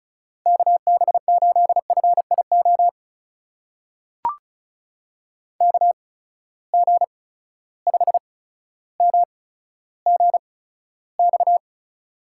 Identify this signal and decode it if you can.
Morse code, heard as a keyed tone, about 36 wpm, spaced as for 15 wpm, 700 Hz. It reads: KB8FIO KG5MGX
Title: USA Call Signs - 6 Characters - 8x Character Spacing - 35wpm